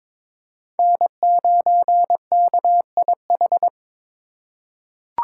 Morse code, heard as a keyed tone, about 21 wpm, 700 Hz. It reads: N9KIH
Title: USA Call Signs - 5 Characters - 4x Character Spacing - 22wpm